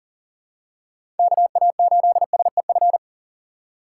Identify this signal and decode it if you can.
Morse code, heard as a keyed tone, about 40 wpm, 700 Hz. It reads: KA8SEF
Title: USA Call Signs - 6 Characters - 2x Character Spacing - 40wpm